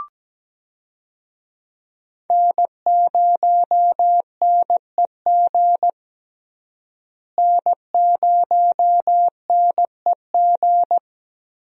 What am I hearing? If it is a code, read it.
Morse code, heard as a keyed tone, about 17 wpm, 700 Hz. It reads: N0NEG N0NEG
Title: USA Call Signs - 5 Characters - Repeated 3x - 17wpm